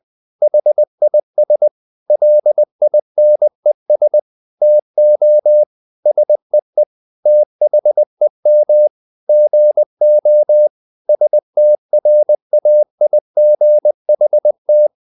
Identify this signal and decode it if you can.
Morse code, heard as a keyed tone, about 20 wpm, 600 Hz. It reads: HIS LINES TO SEE THEM GO STRAIGHT
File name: the old man and the sea / the old man and the sea part 1 20wpm